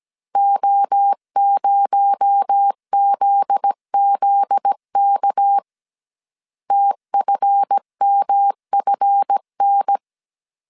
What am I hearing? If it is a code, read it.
Morse code, heard as a keyed tone, about 17 wpm, 800 Hz. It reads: O0ZZK TFMFN
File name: CW / DET / 1 9 1500C S A 16wpm 2828